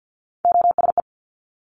Morse code, heard as a keyed tone, about 50 wpm, 700 Hz. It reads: OSE
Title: Top Three Letter Combinations in Words 50wpm